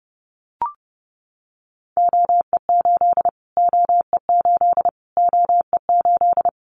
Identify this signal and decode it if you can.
Morse code, heard as a keyed tone, about 30 wpm, 690 Hz. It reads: OE8 OE8 OE8
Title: Callsign Prefixes for Top 25 DXCC - Repeated 3x - 30wpm